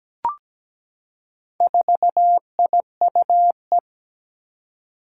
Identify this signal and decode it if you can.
Morse code, heard as a keyed tone, about 17 wpm, 700 Hz. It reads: 4IUE
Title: Four Challenging Letter-Number Combinations 17wpm